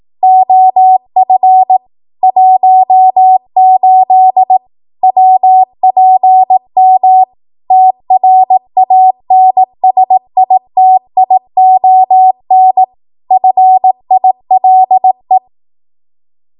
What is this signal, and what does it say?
Morse code, heard as a keyed tone, about 18 wpm, 750 Hz. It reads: OF 18 WPM TRANSITION FILE
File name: Morse / 18WPM